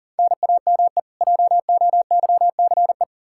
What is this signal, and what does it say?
Morse code, heard as a keyed tone, about 37 wpm, 700 Hz. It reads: NAME JOYCE
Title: K1USN SST QSO Practice 40wpm